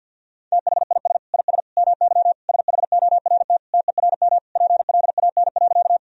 Morse code, heard as a keyed tone, about 52 wpm, 700 Hz. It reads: THIS IS MY SHORT TERM PLAN.
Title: Sentences from Top 1000 Words - Review Part 4 - 50wpm